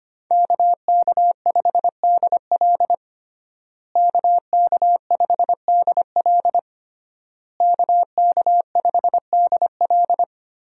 Morse code, heard as a keyed tone, about 25 wpm, 700 Hz. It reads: KK5DL KK5DL KK5DL
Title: USA Call Signs - 5 Characters - Repeated 3x - 25wpm